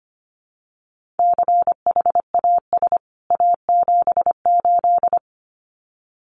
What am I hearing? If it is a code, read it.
Morse code, heard as a keyed tone, about 25 wpm, 700 Hz. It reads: CHAS A78
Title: CWT Exchange 25wpm